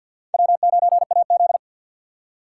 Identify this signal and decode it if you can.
Morse code, heard as a keyed tone, about 62 wpm, 700 Hz. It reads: W9AZ
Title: Speed Racing - USA 1x2 Call Signs 50wpm